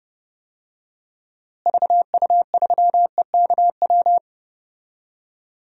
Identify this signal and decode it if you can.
Morse code, heard as a keyed tone, about 30 wpm, 700 Hz. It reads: VU3EKW
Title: DXCC-100 Call Signs - 6 Characters - Repeated 3x - 30wpm